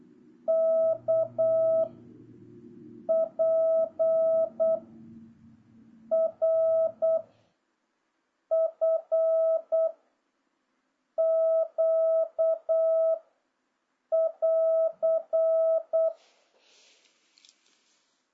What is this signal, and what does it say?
Morse code, heard as a keyed tone, about 8 wpm, 650 Hz. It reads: K P R F Q <AR>
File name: jh8cbh / 2026cw